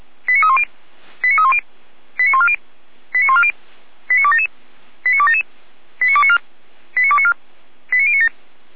Five-tone selective call (ZVEI), heard as quick runs of five tones, tones of about 80 ms. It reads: 89310 89310 89140 89140 89277 89277 89294 89294 89097